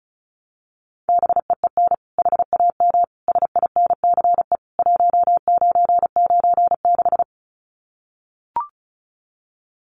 Morse code, heard as a keyed tone, about 34 wpm, 700 Hz. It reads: BEEN HAM SINCE 1996
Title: Common QSO Phrases 35wpm